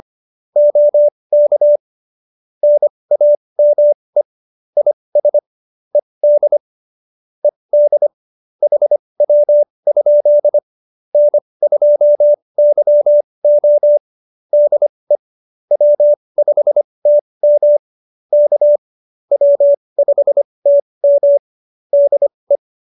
Morse code, heard as a keyed tone, about 25 wpm, spaced as for 20 wpm, 590 Hz. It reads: OK NAME IS ED ED HW? N2YO DE W5TM K W5TM DE
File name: QSO316 20